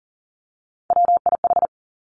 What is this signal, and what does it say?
Morse code, heard as a keyed tone, about 42 wpm, 700 Hz. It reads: WIH
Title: Three Challenging Letter-Number Combinations 40wpm